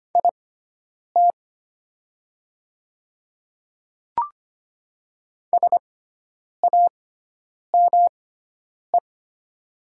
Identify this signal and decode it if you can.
Morse code, heard as a keyed tone, about 25 wpm, spaced as for 9 wpm, 690 Hz. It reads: IT SAME
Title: 6x Character Spacing - Top 200 Words - 25wpm